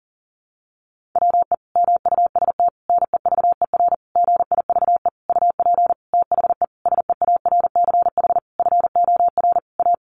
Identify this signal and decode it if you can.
Morse code, heard as a keyed tone, about 40 wpm, 700 Hz. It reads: WE MUST NEVER GIVE UP THE SEARCH FOR A
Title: Sentences from Top 1500 Words 40wpm